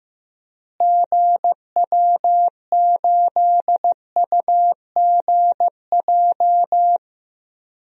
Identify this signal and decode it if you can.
Morse code, heard as a keyed tone, about 15 wpm, 700 Hz. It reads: GW8UGJ
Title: DXCC-100 Call Signs - 6 Characters - Repeated 3x - 15wpm